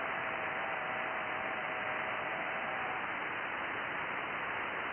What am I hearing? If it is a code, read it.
FT8.
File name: IC-7100